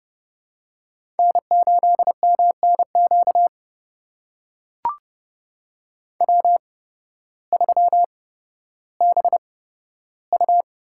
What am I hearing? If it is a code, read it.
Morse code, heard as a keyed tone, about 30 wpm, spaced as for 14 wpm, 700 Hz. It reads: N8MNQ W3BU
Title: USA Call Signs - 8x Character Spacing - 30wpm